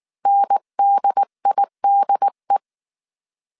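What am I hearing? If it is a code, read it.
Morse code, heard as a keyed tone, about 19 wpm, 800 Hz. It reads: NDIDE
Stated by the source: Morse code dictation at 18 wpm from the IW3SGT QRS Morse code training course, five-character groups of 300 characters.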